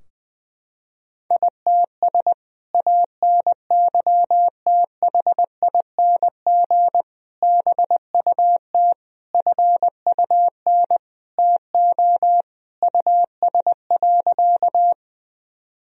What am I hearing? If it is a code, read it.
Morse code, heard as a keyed tone, about 21 wpm, 700 Hz. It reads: ITS ANYTHING BUT FUN TO US.
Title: Sentences from Top 1000 Words - Review Part 4 - 20wpm